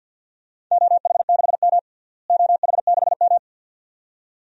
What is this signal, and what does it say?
Morse code, heard as a keyed tone, about 48 wpm, 700 Hz. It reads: OH6M OH6M
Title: DXCC-100 Call Signs - 4 Characters - Repeated 2x - 50wpm